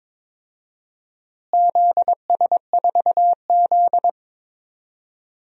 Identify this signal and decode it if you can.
Morse code, heard as a keyed tone, about 22 wpm, 700 Hz. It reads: ZS4Z
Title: DXCC-100 Call Signs - 4 Characters - Repeated 3x - 22wpm